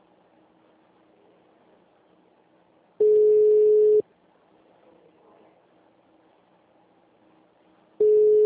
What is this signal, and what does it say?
Ringback tone (Europe, 425 Hz).
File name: out- -2002- - - .2892